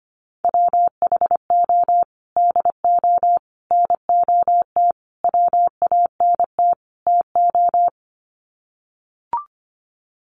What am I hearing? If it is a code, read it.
Morse code, heard as a keyed tone, about 25 wpm, 700 Hz. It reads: WHO DO NOT WANT TO
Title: Sets of 5 Words Encore2 25wpm